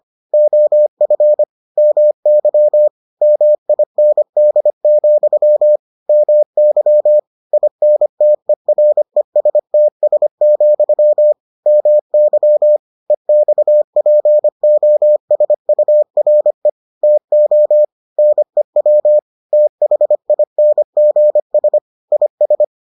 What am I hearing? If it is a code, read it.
Morse code, heard as a keyed tone, about 25 wpm, 600 Hz. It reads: OF MY MIND, MY INTERESTS, MY EXPOSURE TO NEW THINGS IS